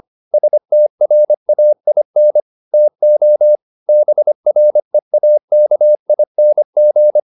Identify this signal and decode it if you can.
Morse code, heard as a keyed tone, about 25 wpm, 600 Hz. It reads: STRAIN TO BREAKING